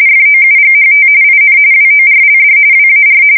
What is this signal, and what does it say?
RTTY, about 45.45 baud. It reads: FOUGHT IN WORLD WAR
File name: fsk170 / USS-New-Jersey